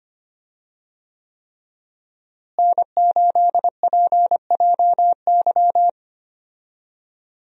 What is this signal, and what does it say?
Morse code, heard as a keyed tone, about 25 wpm, 700 Hz. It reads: N8PJY
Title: USA Call Signs - 5 Characters - Repeated 3x - 25wpm